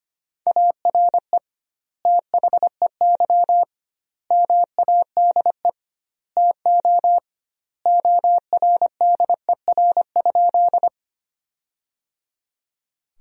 Morse code, heard as a keyed tone, about 25 wpm, 700 Hz. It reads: ARE THEY MADE TO ORDER?
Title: 2x Word-Spacing - Sentences from Top 400 Words - 25wpm